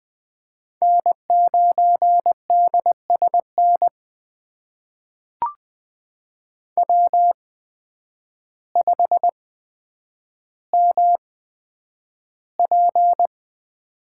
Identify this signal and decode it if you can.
Morse code, heard as a keyed tone, about 20 wpm, spaced as for 9 wpm, 700 Hz. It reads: N9DSN W5MP
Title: USA Call Signs - 8x Character Spacing - 20wpm